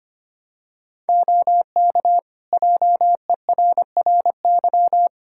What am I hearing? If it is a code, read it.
Morse code, heard as a keyed tone, about 25 wpm, 700 Hz. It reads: OK JERRY
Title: Practice QSO - Exchange 3 - More Info - 25wpm